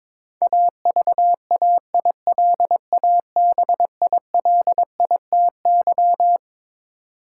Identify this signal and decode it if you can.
Morse code, heard as a keyed tone, about 21 wpm, 700 Hz. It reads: AVAILABILITY
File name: Twelve Letter Words - 22wpm